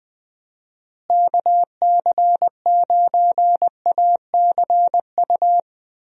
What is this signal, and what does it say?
Morse code, heard as a keyed tone, about 20 wpm, 700 Hz. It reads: KC9ACU
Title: USA Call Signs - 6 Characters - Repeated 3x - 20wpm